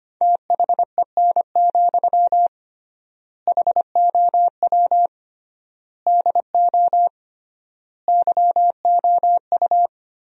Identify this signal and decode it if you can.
Morse code, heard as a keyed tone, about 25 wpm, 700 Hz. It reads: THEN, HOW DO YOU
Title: 3x Word-Spacing - Sentences from Top 100 Words - 25wpm